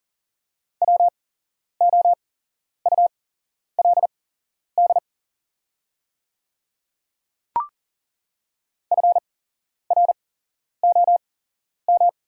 Morse code, heard as a keyed tone, about 40 wpm, spaced as for 11 wpm, 700 Hz. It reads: WOULD FROM
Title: 8x Character Spacing - Top 100 Words - 40wpm